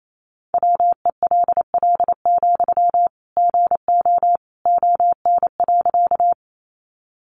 Morse code, heard as a keyed tone, about 28 wpm, 700 Hz. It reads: WELL, GO ON.
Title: Sentences from Top 200 Words 28wpm